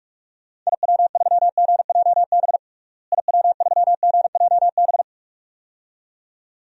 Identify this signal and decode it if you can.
Morse code, heard as a keyed tone, about 46 wpm, 700 Hz. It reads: IW3GJB IW3GJB
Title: DXCC-100 Call Signs - 6 Characters - Repeated 2x - 45wpm